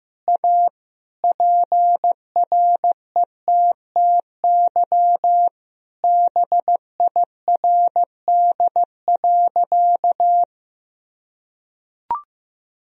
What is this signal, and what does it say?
Morse code, heard as a keyed tone, about 15 wpm, 700 Hz. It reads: A PRETTY BIRD.
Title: Sentences from Top 1000 Words - Review Part 2 - 15wpm